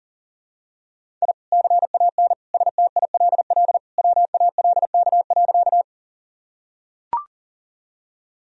Morse code, heard as a keyed tone, about 40 wpm, 700 Hz. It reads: I CAN STILL WALK.
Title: Sentences from Top 1000 Words - Review Part 1 - 40wpm